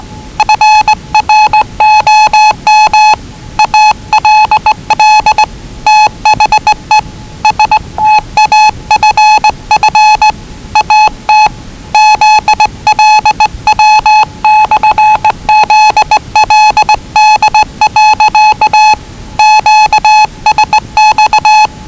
Morse code, heard as a keyed tone, about 20 wpm, 840 Hz. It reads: FROM ALL THE STAFF AT ZLW/ZLD. QSX
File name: WellingtonRadioZlwCloseDownOnHfAndMf / 01-Zlw 4277khz 417.5 qrt until 0925z